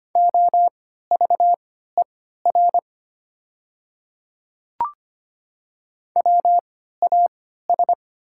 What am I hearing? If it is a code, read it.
Morse code, heard as a keyed tone, about 25 wpm, spaced as for 14 wpm, 700 Hz. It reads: OVER WAS
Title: 3x Character Spacing - Top 100 Words - 25wpm